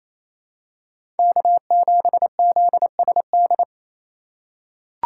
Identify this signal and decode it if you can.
Morse code, heard as a keyed tone, about 28 wpm, 700 Hz. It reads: K7ZSD
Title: USA Call Signs - 8x Character Spacing - 28wpm